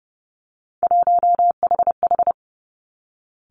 Morse code, heard as a keyed tone, about 30 wpm, 700 Hz. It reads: 1HH
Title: Three Challenging Letter-Number Combinations 30wpm